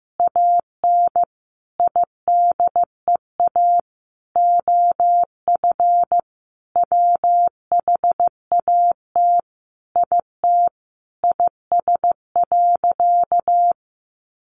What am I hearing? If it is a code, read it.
Morse code, heard as a keyed tone, about 15 wpm, 700 Hz. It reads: AN IDEA OF WHAT IT IS.
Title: Sentences from Top 700 Words 15wpm